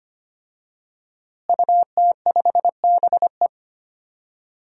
Morse code, heard as a keyed tone, about 25 wpm, 700 Hz. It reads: UT5BE
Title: DXCC-100 Call Signs - 5 Characters - Repeated 2x - 25wpm